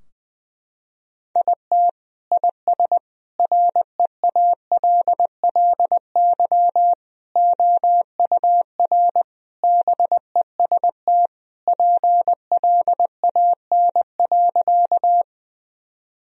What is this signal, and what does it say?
Morse code, heard as a keyed tone, about 21 wpm, 700 Hz. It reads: IT IS REALLY OUR BEST PLAN.
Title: Sentences from Top 1000 Words - Review Part 2 - 20wpm